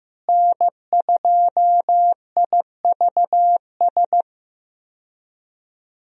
Morse code, heard as a keyed tone, about 15 wpm, 700 Hz. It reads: N2IVS
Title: USA Call Signs - 5 Characters - Repeated 2x - 15wpm